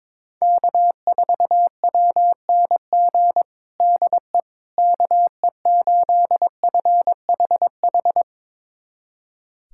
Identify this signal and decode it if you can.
Morse code, heard as a keyed tone, about 22 wpm, 700 Hz. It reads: K4WNG DE KE8FHH
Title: USA Callsign DE Callsign 22wpm